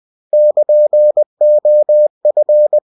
Morse code, heard as a keyed tone, about 20 wpm, 600 Hz. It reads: <KN>OF
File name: cw-books / winnie-the-pooh / winnie-20wpm-page-0104